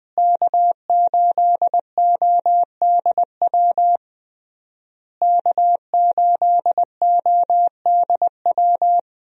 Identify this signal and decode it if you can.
Morse code, heard as a keyed tone, about 20 wpm, 700 Hz. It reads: K8ODW K8ODW
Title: USA Call Signs - 5 Characters - Repeated 3x - 20wpm